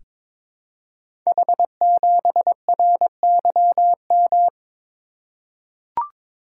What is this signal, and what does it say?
Morse code, heard as a keyed tone, about 22 wpm, 700 Hz. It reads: H7RYM
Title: Code Groups - 5 Letter:Numbers - 22wpm